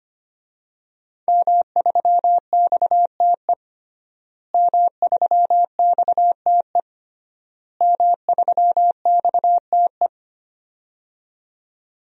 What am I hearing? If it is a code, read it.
Morse code, heard as a keyed tone, about 25 wpm, 700 Hz. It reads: M3XTE M3XTE M3XTE
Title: DXCC-100 Call Signs - 5 Characters - Repeated 3x - 25wpm